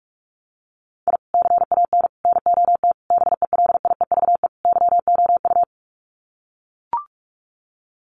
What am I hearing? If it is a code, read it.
Morse code, heard as a keyed tone, about 44 wpm, 700 Hz. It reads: I CAN NOT BELIEVE YOU
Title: Sets of 5 Words Encore 45wpm